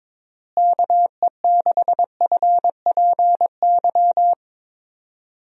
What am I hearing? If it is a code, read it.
Morse code, heard as a keyed tone, about 22 wpm, 700 Hz. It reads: KE6FPY
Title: USA Call Signs - 6 Characters - Repeated 3x - 22wpm